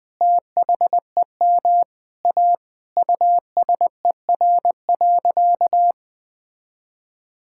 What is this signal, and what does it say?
Morse code, heard as a keyed tone, about 20 wpm, 700 Hz. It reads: THEM A USER.
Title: Sentences from Top 1000 Words - Review Part 5 - 20wpm